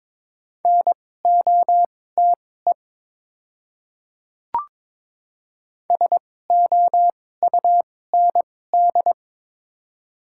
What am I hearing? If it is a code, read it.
Morse code, heard as a keyed tone, about 21 wpm, spaced as for 16 wpm, 700 Hz. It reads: NOTE SOUND
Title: 2x Character Spacing - Top 500 Words - 22wpm